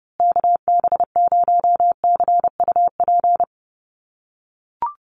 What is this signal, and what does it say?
Morse code, heard as a keyed tone, about 30 wpm, 700 Hz. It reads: KB0CUP
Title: USA Call Signs Repeated 3x 30wpm